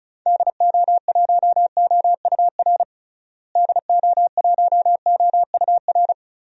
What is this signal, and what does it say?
Morse code, heard as a keyed tone, about 34 wpm, 700 Hz. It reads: DO1OUR DO1OUR
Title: DXCC-100 Call Signs - 6 Characters - Repeated 2x - 35wpm